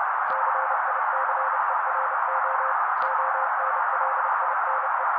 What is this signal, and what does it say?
Morse code, heard as a keyed tone, about 30 wpm, 530 Hz. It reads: CICERO ONLINE
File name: daily / 29wpm noise